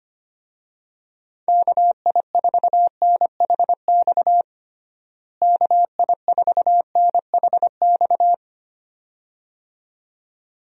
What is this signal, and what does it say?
Morse code, heard as a keyed tone, about 25 wpm, 700 Hz. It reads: KI4NHX KI4NHX
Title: USA Call Signs - 6 Characters - Repeated 2x - 25wpm